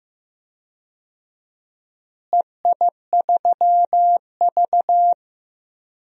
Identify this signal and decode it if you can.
Morse code, heard as a keyed tone, about 15 wpm, 700 Hz. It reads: EI3V
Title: DXCC-100 Call Signs - 4 Characters - Repeated 3x - 15wpm